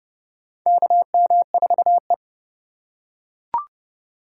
Morse code, heard as a keyed tone, about 30 wpm, 700 Hz. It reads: KM4E
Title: USA Call Signs - 4 Characters - 3x Character Spacing - 30wpm